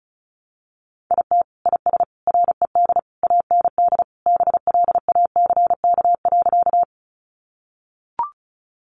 Morse code, heard as a keyed tone, about 34 wpm, 700 Hz. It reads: IT IS RED AND BLACK.
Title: Easy Phrases 35wpm